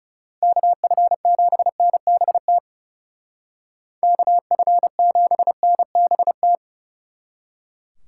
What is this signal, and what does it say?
Morse code, heard as a keyed tone, about 39 wpm, 700 Hz. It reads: KF7NBT KF7NBT